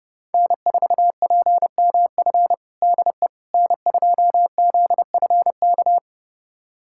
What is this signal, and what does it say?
Morse code, heard as a keyed tone, about 32 wpm, 700 Hz. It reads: N4PMF DE N2ZFK
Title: USA Callsign DE Callsign - 5 Characters - 30wpm